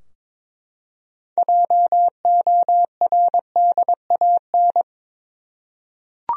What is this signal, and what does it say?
Morse code, heard as a keyed tone, about 22 wpm, 700 Hz. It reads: JORDAN
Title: USA Common Names - 22wpm